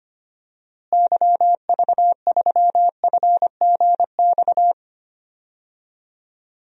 Morse code, heard as a keyed tone, about 25 wpm, 700 Hz. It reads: YV3FGX
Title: DXCC-100 Call Signs - 6 Characters - Repeated 2x - 25wpm